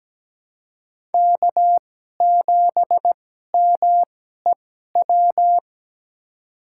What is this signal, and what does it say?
Morse code, heard as a keyed tone, about 17 wpm, spaced as for 12 wpm, 700 Hz. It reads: K7MEW
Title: USA Call Signs - 5 Characters - 2x Character Spacing - 17wpm